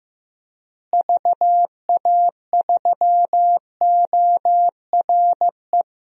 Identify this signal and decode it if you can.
Morse code, heard as a keyed tone, about 15 wpm, 700 Hz. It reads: VA3ORE
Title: DXCC-100 Call Signs - 6 Characters - Repeated 2x - 15wpm